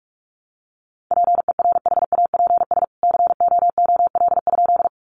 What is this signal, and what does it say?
Morse code, heard as a keyed tone, about 44 wpm, 700 Hz. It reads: PERHAPS COOL?
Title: Sentences from Top 1200 Words 45wpm